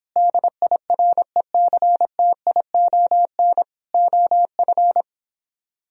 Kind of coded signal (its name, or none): Morse code